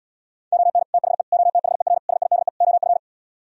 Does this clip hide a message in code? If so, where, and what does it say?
Yes: Morse code from 0.5 to 3.0 s, heard as a keyed tone, about 77 wpm, 700 Hz. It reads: ON THE OTHER SIDE OF